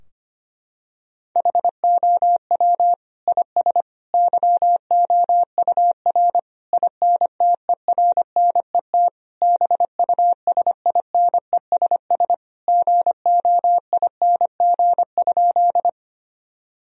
Morse code, heard as a keyed tone, about 25 wpm, 700 Hz. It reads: HOW IS YOUR INTERNET BUSINESS GOING?
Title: Sentences from Top 1000 Words - Review Part 4 - 25wpm